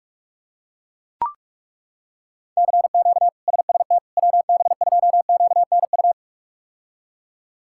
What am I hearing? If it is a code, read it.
Morse code, heard as a keyed tone, about 42 wpm, 700 Hz. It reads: CQ SST WB2QNU
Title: K1USN SST QSO Practice - 45wpm